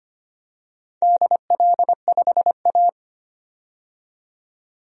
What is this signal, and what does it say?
Morse code, heard as a keyed tone, about 25 wpm, 700 Hz. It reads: DL5A
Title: DXCC-100 Call Signs - 4 Characters - 25wpm